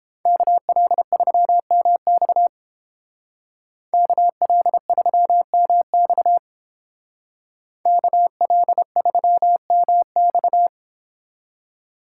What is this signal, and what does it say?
Morse code, heard as a keyed tone, about 30 wpm, 700 Hz. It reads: KL3MX KL3MX KL3MX